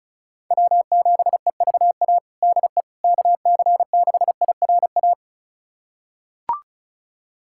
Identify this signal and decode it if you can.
Morse code, heard as a keyed tone, about 36 wpm, 700 Hz. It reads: W7EVA DE KC6IRA
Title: USA Callsign DE Callsign 35wpm